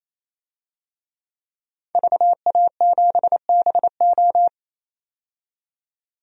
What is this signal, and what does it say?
Morse code, heard as a keyed tone, about 28 wpm, 700 Hz. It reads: VA7BO
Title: DXCC-100 Call Signs - 5 Characters - 28wpm